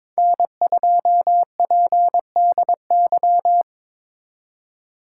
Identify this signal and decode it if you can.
Morse code, heard as a keyed tone, about 21 wpm, 700 Hz. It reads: N2PDY
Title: USA Call Signs - 5 Characters - 2x Character Spacing - 22wpm